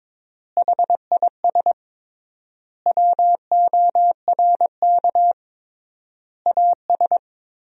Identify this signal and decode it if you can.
Morse code, heard as a keyed tone, about 22 wpm, 700 Hz. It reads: HIS WORK AS
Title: 3x Word-Spacing - Sentences from Top 500 Words - 22wpm